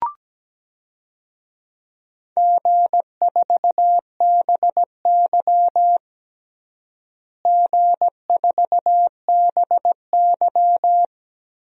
Morse code, heard as a keyed tone, about 17 wpm, 700 Hz. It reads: G4BY G4BY
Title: DXCC-100 Call Signs - 4 Characters - Repeated 3x - 17wpm